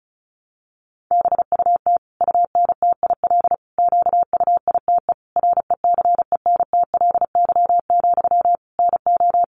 Morse code, heard as a keyed tone, about 34 wpm, 700 Hz. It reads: BUT UNTIL QUITE RECENTLY, NO
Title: Sentences from Top 1200 Words 35wpm